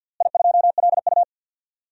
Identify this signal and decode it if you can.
Morse code, heard as a keyed tone, about 49 wpm, 700 Hz. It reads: I2LU
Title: DXCC-100 Call Signs - 4 Characters - Repeated 2x - 50wpm